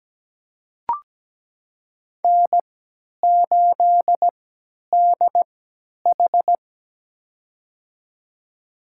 Morse code, heard as a keyed tone, about 17 wpm, spaced as for 10 wpm, 700 Hz. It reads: N8DH
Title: USA Call Signs - 3x Character Spacing - 17wpm